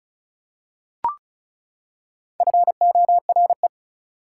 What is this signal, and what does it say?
Morse code, heard as a keyed tone, about 36 wpm, 700 Hz. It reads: FORE
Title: Prefixes and Suffixes 35wpm